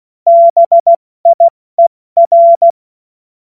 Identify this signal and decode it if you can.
Morse code, heard as a keyed tone, about 16 wpm, 680 Hz. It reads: BIER
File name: Morsekurs / Witze / Witz 22. Februar 2022 70 F80